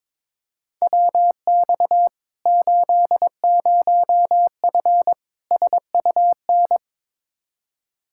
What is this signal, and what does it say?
Morse code, heard as a keyed tone, about 22 wpm, 700 Hz. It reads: WX 80F SUN
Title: Practice QSO - Exchange 2 - Info - 22wpm